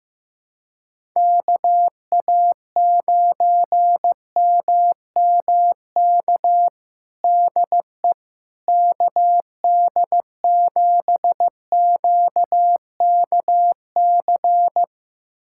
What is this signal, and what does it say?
Morse code, heard as a keyed tone, about 15 wpm, 700 Hz. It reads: KA9MMK DE KD7QKC
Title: USA Callsign DE Callsign 15wpm